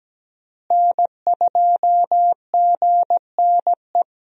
Morse code, heard as a keyed tone, about 17 wpm, 700 Hz. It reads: N2GNE
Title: USA Call Signs - 5 Characters - Repeated 3x - 17wpm